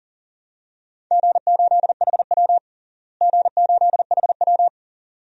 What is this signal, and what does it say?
Morse code, heard as a keyed tone, about 40 wpm, 700 Hz. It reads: G8HW G8HW
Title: DXCC-100 Call Signs - 4 Characters - Repeated 2x - 40wpm